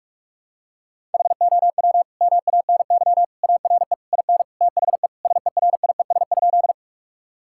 Morse code, heard as a keyed tone, about 41 wpm, 700 Hz. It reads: HOW MANY ARE IN THE SERIES?
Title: Sentences from Top 1000 Words - Review Part 5 - 45wpm